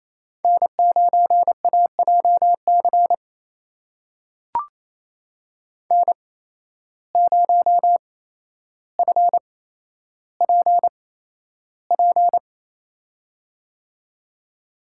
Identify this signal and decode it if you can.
Morse code, heard as a keyed tone, about 28 wpm, spaced as for 12 wpm, 700 Hz. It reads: N9AJC N0FPP
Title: USA Call Signs - 5 Characters - 8x Character Spacing - 28wpm